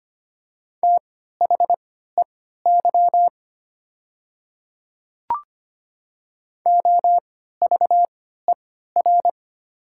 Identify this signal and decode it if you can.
Morse code, heard as a keyed tone, about 25 wpm, spaced as for 14 wpm, 700 Hz. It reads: THEY OVER
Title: 3x Character Spacing - Top 100 Words - 25wpm